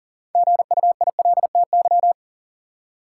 Morse code, heard as a keyed tone, about 40 wpm, 700 Hz. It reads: GUILTY
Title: Six Letter Words - 40wpm